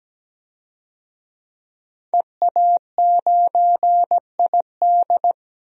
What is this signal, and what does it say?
Morse code, heard as a keyed tone, about 17 wpm, 700 Hz. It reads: EA9ID
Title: DXCC-100 Call Signs - 5 Characters - 17wpm